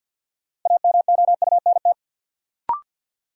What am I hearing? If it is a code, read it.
Morse code, heard as a keyed tone, about 53 wpm, 700 Hz. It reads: AMOUNT
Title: Sets of 3 Words - New Word Drill - 50wpm